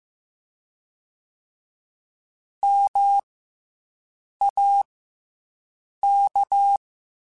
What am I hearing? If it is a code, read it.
Morse code, heard as a keyed tone, about 15 wpm, 800 Hz. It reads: M A K
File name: Special INGs / Special INGs-mp3s / SlowToFaster ing @13